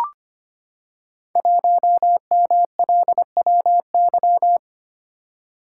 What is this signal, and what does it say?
Morse code, heard as a keyed tone, about 25 wpm, 700 Hz. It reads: 1MLWY
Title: Code Groups - 5 Letter:Numbers - 25wpm